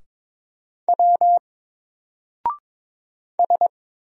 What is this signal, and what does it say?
Morse code, heard as a keyed tone, about 22 wpm, 700 Hz. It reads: W S
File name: Single-Letter-Number-22wpm